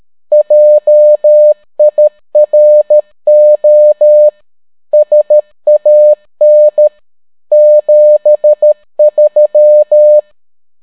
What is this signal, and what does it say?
Morse code, heard as a keyed tone, about 13 wpm, 600 Hz. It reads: JIRO SAN 73